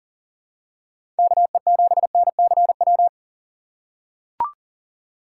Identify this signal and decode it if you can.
Morse code, heard as a keyed tone, about 40 wpm, 700 Hz. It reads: KE7NCW
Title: USA Call Signs - 6 Characters - 2x Character Spacing - 40wpm